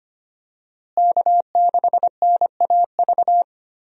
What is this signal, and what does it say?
Morse code, heard as a keyed tone, about 25 wpm, 700 Hz. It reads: K6NAV
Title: USA Call Signs - 5 Characters - Repeated 3x - 25wpm